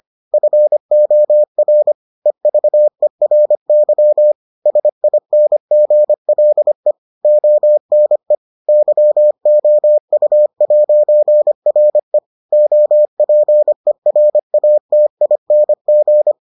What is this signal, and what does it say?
Morse code, heard as a keyed tone, about 25 wpm, 600 Hz. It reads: FOR EVERY SINGLE ONE YOU'RE OPERATING